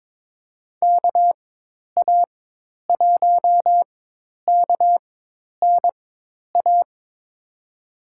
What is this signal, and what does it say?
Morse code, heard as a keyed tone, about 23 wpm, spaced as for 10 wpm, 700 Hz. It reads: KA1KNA
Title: USA Call Signs - 6 Characters - 4x Character Spacing - 22wpm